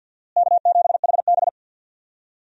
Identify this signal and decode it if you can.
Morse code, heard as a keyed tone, about 51 wpm, 700 Hz. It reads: K7HB
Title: USA Call Signs - 4 Characters - Repeated 3x - 50wpm